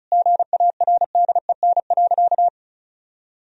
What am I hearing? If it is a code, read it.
Morse code, heard as a keyed tone, about 34 wpm, 700 Hz. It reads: GARDEN.
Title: Sentences from Top 1000 Words - Review Part 4 - 35wpm